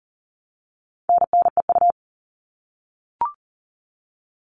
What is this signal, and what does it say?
Morse code, heard as a keyed tone, about 40 wpm, 700 Hz. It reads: NNEU
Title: All Cut Numbers - 4-Digit 40wpm